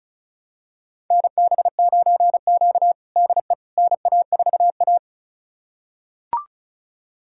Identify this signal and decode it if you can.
Morse code, heard as a keyed tone, about 36 wpm, 700 Hz. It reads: NB9Q DE NA4A
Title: USA Callsign DE Callsign - 4 Characters - 35wpm